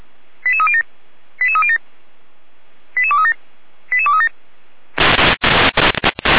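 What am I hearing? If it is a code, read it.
Five-tone selective call (ZVEI), heard as quick runs of five tones, tones of about 70 ms. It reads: 88307 88307 88237 88237
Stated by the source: ZVEI five-tone call received by SDR, from audio.f85360.feuerwehr-orlamuende.org fire brigade server.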